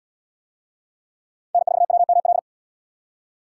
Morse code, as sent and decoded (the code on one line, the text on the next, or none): -. ....- .-- -- .-..
N4WML